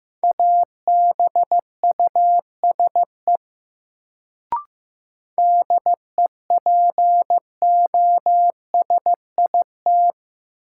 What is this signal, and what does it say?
Morse code, heard as a keyed tone, about 15 wpm, 700 Hz. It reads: ABUSE DEPOSIT
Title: Sentences from Top 1300 Words - New Word Drill Alt - 15wpm